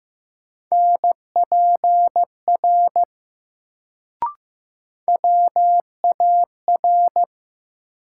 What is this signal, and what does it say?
Morse code, heard as a keyed tone, about 15 wpm, 700 Hz. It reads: NPR WAR
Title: Sets of 5 Words - New Word Drill - 15wpm